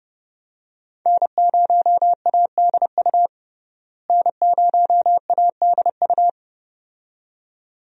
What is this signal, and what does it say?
Morse code, heard as a keyed tone, about 31 wpm, 700 Hz. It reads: N0ADU N0ADU
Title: USA Call Signs - 5 Characters - Repeated 2x - 30wpm